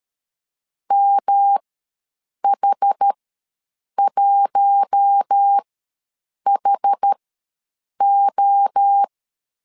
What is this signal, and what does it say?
Morse code, heard as a keyed tone, about 13 wpm, spaced as for 7 wpm, 800 Hz. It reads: MH1HO
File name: LEZIONI / 5 LBVC / 5 L ws 12wpm 2958